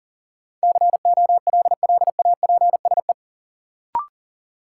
Morse code, heard as a keyed tone, about 40 wpm, 700 Hz. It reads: COLLAPSE
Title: Sets of 5 Words Encore - New Word Drill - 40wpm